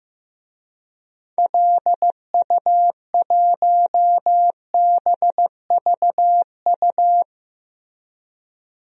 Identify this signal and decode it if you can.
Morse code, heard as a keyed tone, about 15 wpm, 700 Hz. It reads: LU1BVU
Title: DXCC-100 Call Signs - 6 Characters - Repeated 3x - 15wpm